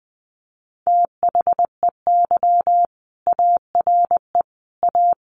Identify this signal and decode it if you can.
Morse code, heard as a keyed tone, about 19 wpm, 700 Hz. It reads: THEY ARE A
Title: Sentences from Top 1500 Words 20wpm